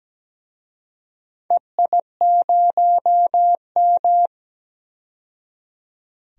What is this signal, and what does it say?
Morse code, heard as a keyed tone, about 17 wpm, 700 Hz. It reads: EI0M
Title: DXCC-100 Call Signs - 4 Characters - 17wpm